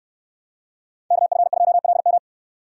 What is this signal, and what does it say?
Morse code, heard as a keyed tone, about 69 wpm, 700 Hz. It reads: KF2LM